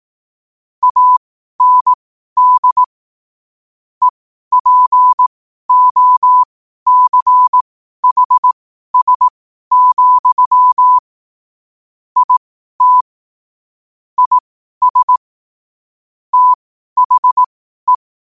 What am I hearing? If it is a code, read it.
Morse code, heard as a keyed tone, about 18 wpm, spaced as for 12 wpm, 1000 Hz. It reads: AND EPOCHS, IT IS THE
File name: qotdfiles / 12wpm